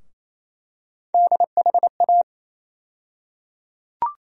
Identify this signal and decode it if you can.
Morse code, heard as a keyed tone, about 28 wpm, 700 Hz. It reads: DHA
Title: USA Call Sign Postfixes - 3 Letters 28wpm